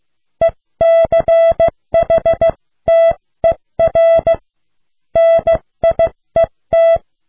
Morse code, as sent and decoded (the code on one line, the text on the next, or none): . -.-. .... - . .-. / -. .. -
ECHTER NIET